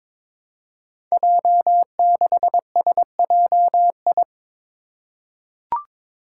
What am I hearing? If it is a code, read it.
Morse code, heard as a keyed tone, about 22 wpm, 700 Hz. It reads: J6SJI
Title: Five Challenging Letter-Number Combinations 22wpm